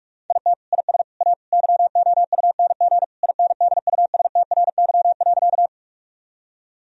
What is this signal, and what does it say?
Morse code, heard as a keyed tone, about 46 wpm, 700 Hz. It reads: IT IS A YOUNG INDUSTRY.
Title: Sentences from Top 1000 Words - Review Part 4 - 45wpm